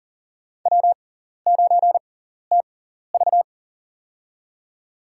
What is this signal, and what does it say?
Morse code, heard as a keyed tone, about 40 wpm, spaced as for 14 wpm, 700 Hz. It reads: W9TV
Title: USA Call Signs - 4 Characters - 6x Character Spacing - 40wpm